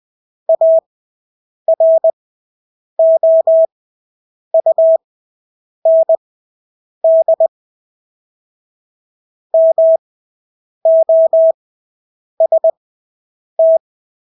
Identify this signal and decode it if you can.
Morse code, heard as a keyed tone, about 20 wpm, spaced as for 8 wpm, 650 Hz. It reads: AROUND MOST
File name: Engelske ord 20/8 wpm